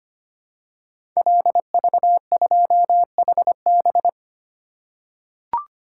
Morse code, heard as a keyed tone, about 25 wpm, 700 Hz. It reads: LV2HB
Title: Code Groups - 5 Letter:Numbers - 25wpm